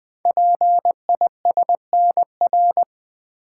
Morse code, heard as a keyed tone, about 20 wpm, 700 Hz. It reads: PISNR
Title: Code Groups - 5 Letters - 20wpm